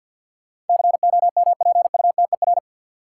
Morse code, heard as a keyed tone, about 48 wpm, 700 Hz. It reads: COMPUTER
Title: Sentences from Top 1000 Words - Review Part 3 - New Word Drill - 50wpm